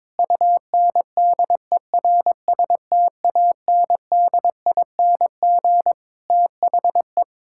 Morse code, heard as a keyed tone, about 22 wpm, 700 Hz. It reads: UNDERSTANDING THE
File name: Sentences with 9-Word Phrases - 22wpm